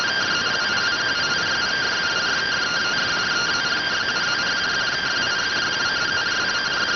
Two-tone teleprinter signal, about 170 Hz shift.